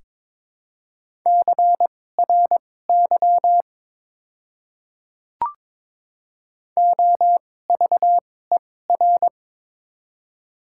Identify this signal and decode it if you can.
Morse code, heard as a keyed tone, about 22 wpm, spaced as for 16 wpm, 700 Hz. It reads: CRY OVER
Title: 2x Character Spacing - Top 500 Words - 22wpm